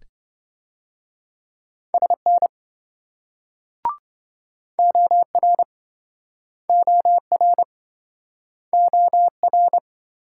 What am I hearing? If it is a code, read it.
Morse code, heard as a keyed tone, about 26 wpm, 700 Hz. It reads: SN OR OR OR